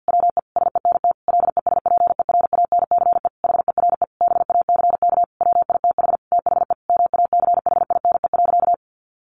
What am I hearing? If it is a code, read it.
Morse code, heard as a keyed tone, about 50 wpm, 700 Hz. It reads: WE SENT LESPERANCE HERE BACK WITH THE MACHINE.
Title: A Sound Of Thunder 50wpm